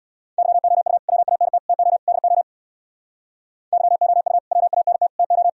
Morse code, heard as a keyed tone, about 60 wpm, 700 Hz. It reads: YOU WANT TO DO YOU WANT TO
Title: Speed-Racing - Sets of 4 Words 50wpm